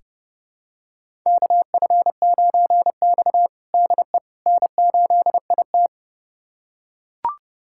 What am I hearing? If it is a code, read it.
Morse code, heard as a keyed tone, about 30 wpm, 700 Hz. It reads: KF9X DE N8IT
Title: USA Callsign DE Callsign - 4 Characters - 30wpm